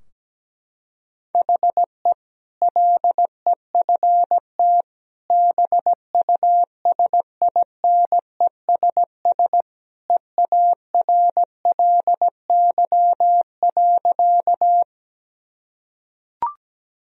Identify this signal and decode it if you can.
Morse code, heard as a keyed tone, about 17 wpm, 700 Hz. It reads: HE LEFT BUSINESS EARLY.
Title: Sentences from Top 1000 Words - Review Part 2 - 17wpm